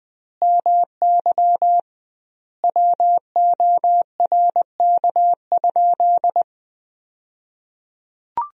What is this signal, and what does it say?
Morse code, heard as a keyed tone, about 20 wpm, 700 Hz. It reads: MY WORK?
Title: 2x Word-Spacing - Sentences from Top 500 Words - 20wpm